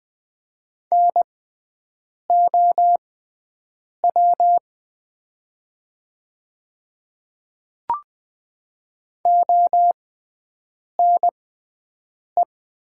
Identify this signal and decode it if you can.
Morse code, heard as a keyed tone, about 20 wpm, spaced as for 7 wpm, 700 Hz. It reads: NOW ONE
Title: 6x Character Spacing - Top 100 Words - 20wpm